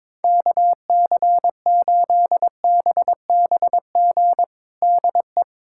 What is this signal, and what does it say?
Morse code, heard as a keyed tone, about 22 wpm, 700 Hz. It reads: KC8BBG DE
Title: USA Callsign DE Callsign 22wpm